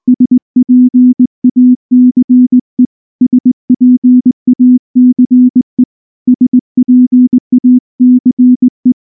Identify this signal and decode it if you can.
Morse code, heard as a keyed tone, about 19 wpm, 250 Hz. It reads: SPACE SPACE SPACE
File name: morse / morse-training-20WPM